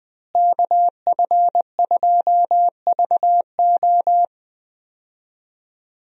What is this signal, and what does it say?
Morse code, heard as a keyed tone, about 20 wpm, 700 Hz. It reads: KF2VO
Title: USA Call Signs - 5 Characters - Repeated 3x - 20wpm